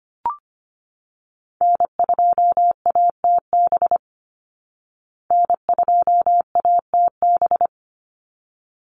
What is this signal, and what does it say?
Morse code, heard as a keyed tone, about 25 wpm, 700 Hz. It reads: N2ATB N2ATB
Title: Call Signs Repeated 3x - 25wpm